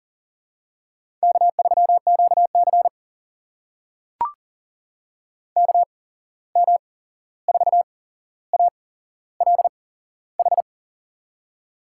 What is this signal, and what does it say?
Morse code, heard as a keyed tone, about 38 wpm, spaced as for 15 wpm, 700 Hz. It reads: K3QC KM4ALH
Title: USA Call Signs - 8x Character Spacing - 40wpm